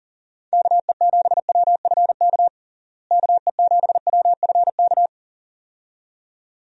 Morse code, heard as a keyed tone, about 40 wpm, 700 Hz. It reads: KE7WFK KE7WFK
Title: USA Call Signs - 6 Characters - Repeated 2x - 40wpm